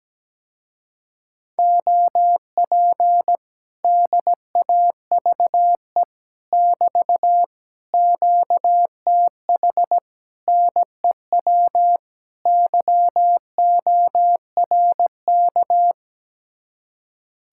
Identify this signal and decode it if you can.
Morse code, heard as a keyed tone, about 17 wpm, 700 Hz. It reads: OP DAVE <BT> QTH NEW YORK
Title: Speed-Racing Common QSO Phrases - 17wpm